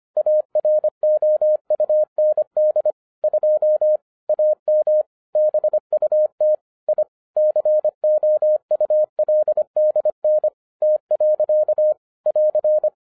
Morse code, heard as a keyed tone, about 25 wpm, 610 Hz. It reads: AROUND 2 AM BUT I COULDN T. <AR>
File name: 2011-07-mp3 / n34 THE 3.11 2 25wpm